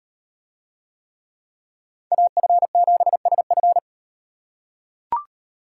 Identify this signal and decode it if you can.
Morse code, heard as a keyed tone, about 38 wpm, 700 Hz. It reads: AF7SF